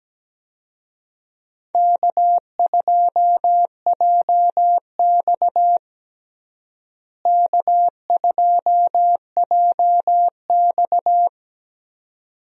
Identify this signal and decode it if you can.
Morse code, heard as a keyed tone, about 17 wpm, 700 Hz. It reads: K2JX K2JX
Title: USA Call Signs - 4 Characters - Repeated 2x - 17wpm